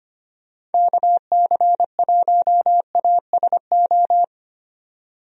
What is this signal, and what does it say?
Morse code, heard as a keyed tone, about 25 wpm, 700 Hz. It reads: KC1ASO
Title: USA Call Signs - 6 Characters - 25wpm